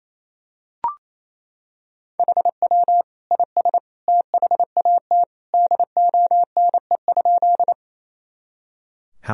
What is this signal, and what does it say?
Morse code, heard as a keyed tone, about 28 wpm, 700 Hz. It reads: HW IS THAT DONE?
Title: Sentences from Top 500 Words and Abbreviations 28wpm